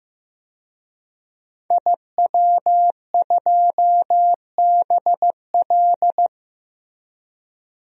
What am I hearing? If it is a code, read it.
Morse code, heard as a keyed tone, about 15 wpm, 700 Hz. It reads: IW2BL
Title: DXCC-100 Call Signs - 5 Characters - Repeated 3x - 15wpm